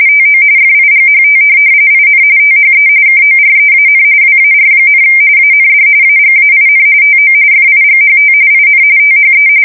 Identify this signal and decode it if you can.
RTTY, about 45.45 baud. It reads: TO FIGHT IN THE KOREAN WAR, AND AGAIN TO FIGHT IN THE
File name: fsk170 / USS-New-Jersey